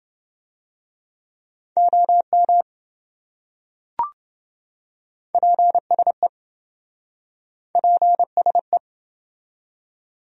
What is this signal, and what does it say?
Morse code, heard as a keyed tone, about 27 wpm, 700 Hz. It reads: OM PSE PSE